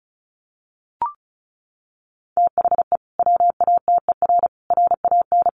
Morse code, heard as a keyed tone, about 34 wpm, 700 Hz. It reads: THE WATER RAN
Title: Sentences from Top 500 Words 35wpm